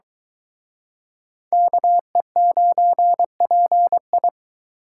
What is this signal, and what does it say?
Morse code, heard as a keyed tone, about 24 wpm, 700 Hz. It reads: KE9PI